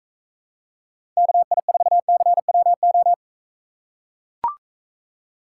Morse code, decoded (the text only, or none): KI4CWO